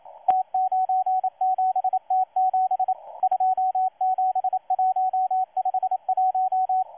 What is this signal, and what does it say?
Morse code, heard as a keyed tone, about 27 wpm, 750 Hz, with radio noise. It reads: T97T7 27151
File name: M12-2016-07-07-1310utc-13926khz-error-restart